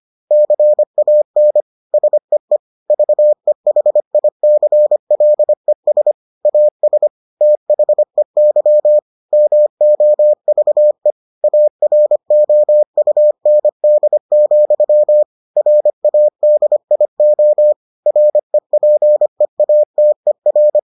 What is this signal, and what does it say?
Morse code, heard as a keyed tone, about 25 wpm, 600 Hz. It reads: CAN SEE VEHICLES AS THEY MOVE AROUND, RADIO REPEATER